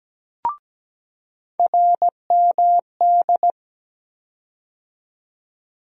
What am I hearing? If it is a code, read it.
Morse code, heard as a keyed tone, about 17 wpm, 700 Hz. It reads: RMD
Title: USA Call Sign Postfixes - 3 Letters 17wpm